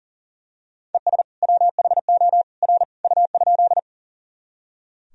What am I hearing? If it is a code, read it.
Morse code, heard as a keyed tone, about 38 wpm, 700 Hz. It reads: ES WHO R U?
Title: Sentences from Top 100 Words and Abbreviations 40wpm